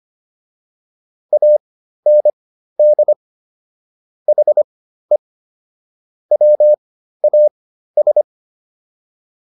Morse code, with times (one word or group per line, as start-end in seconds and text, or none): 1.3-3.1 AND
4.3-5.2 HE
6.3-8.2 WAS